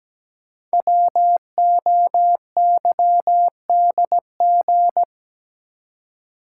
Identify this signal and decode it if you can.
Morse code, heard as a keyed tone, about 17 wpm, 700 Hz. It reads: WOYDG
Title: Code Groups - 5 Letters - 17wpm